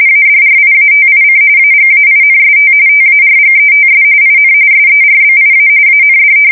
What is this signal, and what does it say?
RTTY, about 45.45 baud. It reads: ADRIATIC AGAINST TARGETS IN SERBIA AND
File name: fsk170 / usnavy5